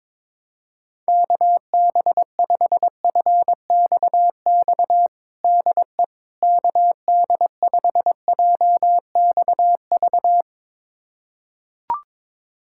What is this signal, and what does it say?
Morse code, heard as a keyed tone, about 22 wpm, 700 Hz. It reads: KB5FXX DE KD5JXV
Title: USA Callsign DE Callsign - 6 Characters - 22wpm